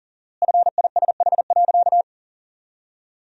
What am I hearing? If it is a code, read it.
Morse code, heard as a keyed tone, about 40 wpm, 700 Hz. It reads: FISH.
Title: 4x Word-Spacing - Sentences from Top 300 Words - 40wpm